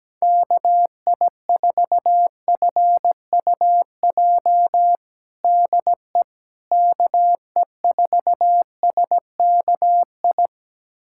Morse code, heard as a keyed tone, about 17 wpm, 700 Hz. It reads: KI4FUJ DE KE4SKI
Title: USA Callsign DE Callsign - 6 Characters - 17wpm